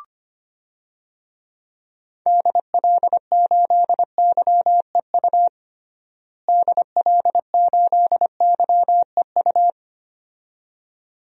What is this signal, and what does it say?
Morse code, heard as a keyed tone, about 25 wpm, 700 Hz. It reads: DL8YEU DL8YEU
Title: DXCC-100 Call Signs - 6 Characters - Repeated 2x - 25wpm